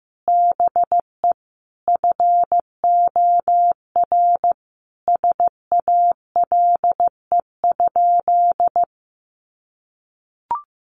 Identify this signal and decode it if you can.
Morse code, heard as a keyed tone, about 15 wpm, 700 Hz. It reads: BE FOR SALE?
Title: Sentences from Top 1400 Words 15wpm